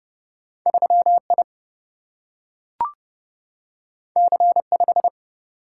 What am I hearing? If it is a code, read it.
Morse code, heard as a keyed tone, about 30 wpm, 700 Hz. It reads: 3I C5
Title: Single-Letter Single-Number 30wpm